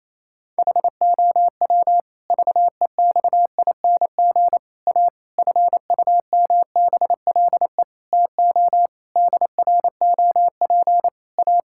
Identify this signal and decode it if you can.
Morse code, heard as a keyed tone, about 28 wpm, 700 Hz. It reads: HOW VEXING A FUMBLE TO DROP A
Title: Pangram Challenge v2 - 28wpm